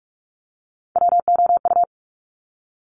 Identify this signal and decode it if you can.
Morse code, heard as a keyed tone, about 46 wpm, 700 Hz. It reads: WOU
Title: Top Three Letter Combinations in Words 45wpm